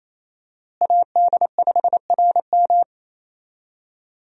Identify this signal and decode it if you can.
Morse code, heard as a keyed tone, about 28 wpm, 700 Hz. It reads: AD5RM
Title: USA Call Signs - 5 Characters - Repeated 2x - 28wpm